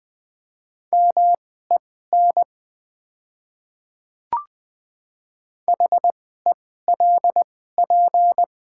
Morse code, heard as a keyed tone, about 20 wpm, spaced as for 15 wpm, 700 Hz. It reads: MEN HELP
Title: 2x Character Spacing - Top 300 Words - 20wpm